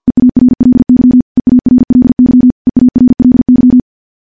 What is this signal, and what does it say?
Morse code, heard as a keyed tone, about 50 wpm, 250 Hz. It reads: EARLY EARLY EARLY
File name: morse / morse-training-50WPM